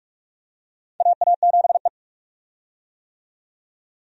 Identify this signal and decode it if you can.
Morse code, heard as a keyed tone, about 44 wpm, 700 Hz. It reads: AA7E
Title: USA Call Signs - 4 Characters - 45wpm